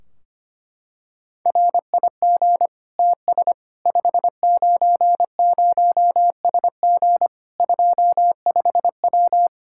Morse code, heard as a keyed tone, about 25 wpm, 700 Hz. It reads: RIG TS 590SG 25W
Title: Practice QSO - Exchange 2 - Info - 25wpm